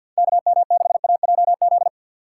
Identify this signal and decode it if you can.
Morse code, heard as a keyed tone, about 47 wpm, 700 Hz. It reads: KM6AJZ
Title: USA Call Signs - 3x Character Spacing - 50wpm